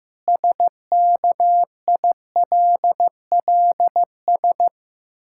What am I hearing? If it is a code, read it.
Morse code, heard as a keyed tone, about 15 wpm, 700 Hz. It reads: SKILLS
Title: Pangram Challenge v2 - 15wpm